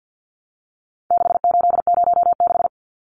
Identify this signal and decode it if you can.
Morse code, heard as a keyed tone, about 50 wpm, 700 Hz. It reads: 6806
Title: Numbers - 4-Digit 50wpm